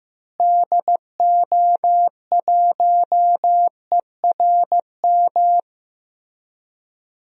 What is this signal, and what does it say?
Morse code, heard as a keyed tone, about 15 wpm, 700 Hz. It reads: DO1ERM
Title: DXCC-100 Call Signs - 6 Characters - Repeated 3x - 15wpm